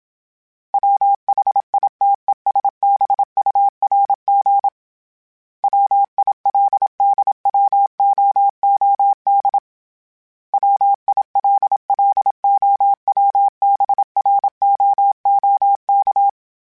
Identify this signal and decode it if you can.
Morse code, heard as a keyed tone, about 27 wpm, 800 Hz. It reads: WHITESBURG WILDWOOD WILLOWBROOK
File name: fromSMs / FromSMs mp3s / Cities@25